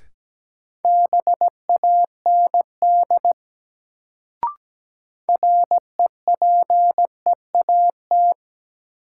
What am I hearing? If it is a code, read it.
Morse code, heard as a keyed tone, about 17 wpm, 700 Hz. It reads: BAND REPEAT
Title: Sentences from Top 1600 Words - New Word Drill Alt - 17wpm